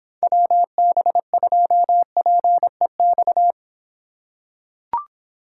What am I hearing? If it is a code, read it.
Morse code, heard as a keyed tone, about 26 wpm, 700 Hz. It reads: WB2PEX